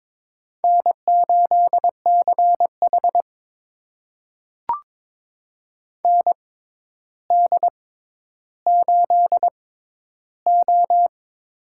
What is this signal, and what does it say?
Morse code, heard as a keyed tone, about 23 wpm, spaced as for 11 wpm, 700 Hz. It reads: N8CH ND8O
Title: USA Call Signs - 4 Characters - 6x Character Spacing - 22wpm